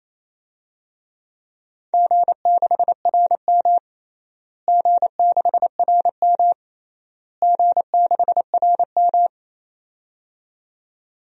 Morse code, heard as a keyed tone, about 28 wpm, 700 Hz. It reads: G6RM G6RM G6RM
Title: DXCC-100 Call Signs - 4 Characters - Repeated 3x - 28wpm